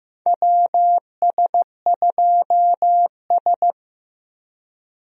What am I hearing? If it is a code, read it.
Morse code, heard as a keyed tone, about 15 wpm, 700 Hz. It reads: WS2S
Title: USA Call Signs - 4 Characters - 15wpm